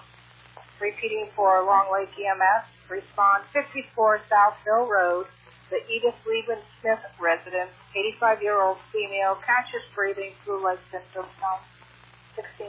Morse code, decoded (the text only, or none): none